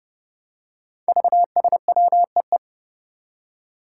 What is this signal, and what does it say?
Morse code, heard as a keyed tone, about 30 wpm, 700 Hz. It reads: VSWEE
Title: Five Challenging Letter-Number Combinations 30wpm